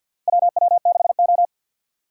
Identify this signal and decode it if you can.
Morse code, heard as a keyed tone, about 50 wpm, 700 Hz. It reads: WW6O
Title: USA Call Signs - 4 Characters - 2x Character Spacing - 50wpm